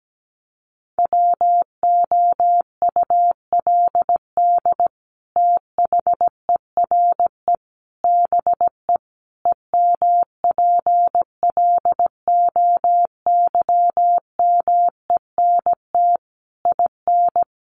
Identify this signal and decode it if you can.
Morse code, heard as a keyed tone, about 17 wpm, 700 Hz. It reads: WOULD THERE BE EMPLOYMENT IN
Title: Sentences from Top 1500 Words 17wpm